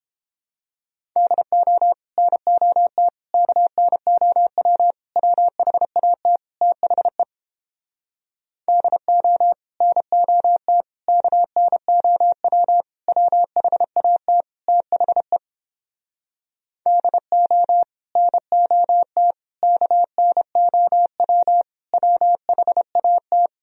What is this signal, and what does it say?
Morse code, heard as a keyed tone, about 30 wpm, 700 Hz. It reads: DO NOT KNOW WHAT THE DO NOT KNOW WHAT THE DO NOT KNOW WHAT